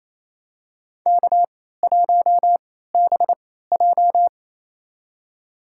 Morse code, heard as a keyed tone, about 28 wpm, spaced as for 16 wpm, 700 Hz. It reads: K1BJ
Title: USA Call Signs - 4 Characters - 3x Character Spacing - 28wpm